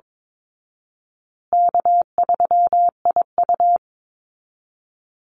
Morse code, heard as a keyed tone, about 22 wpm, 700 Hz. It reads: K3IU
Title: Easy Call Signs Repeated 3x - 22wpm